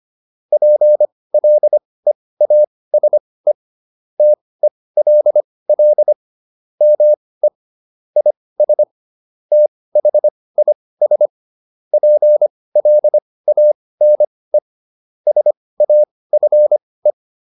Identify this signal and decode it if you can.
Morse code, heard as a keyed tone, about 25 wpm, spaced as for 18 wpm, 600 Hz. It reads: PLEASE TELL ME IS THIS PLANE SAFE